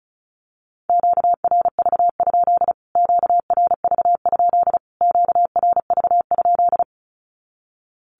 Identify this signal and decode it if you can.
Morse code, heard as a keyed tone, about 34 wpm, 700 Hz. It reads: QRV? QRV? QRV?
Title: Q-Codes Repeated 3x 35wpm